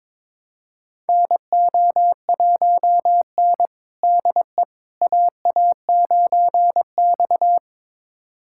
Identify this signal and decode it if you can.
Morse code, heard as a keyed tone, about 22 wpm, 700 Hz. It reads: NO1N DE AA9X
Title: USA Callsign DE Callsign - 4 Characters - 22wpm